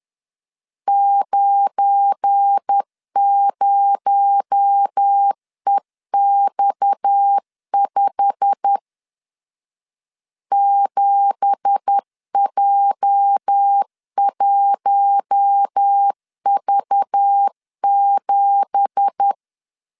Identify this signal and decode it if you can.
Morse code, heard as a keyed tone, about 10 wpm, 800 Hz. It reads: 90EX5 7J1V7
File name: LEZIONI / 9 2867 10wpm 2733